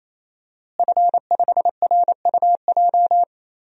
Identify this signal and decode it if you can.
Morse code, heard as a keyed tone, about 28 wpm, 700 Hz. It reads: F5RUJ
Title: DXCC-100 Call Signs - 5 Characters - Repeated 3x - 28wpm